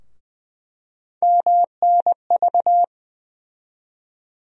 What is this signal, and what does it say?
Morse code, heard as a keyed tone, about 20 wpm, 700 Hz. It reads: MNV
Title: USA Call Sign Postfixes - 3 Letters 20wpm